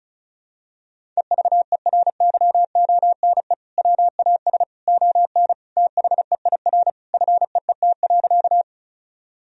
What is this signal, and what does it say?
Morse code, heard as a keyed tone, about 36 wpm, 700 Hz. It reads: EVERYONE WAS ON THEIR FEET.
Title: Sentences from Top 1000 Words - Review Part 1 - 35wpm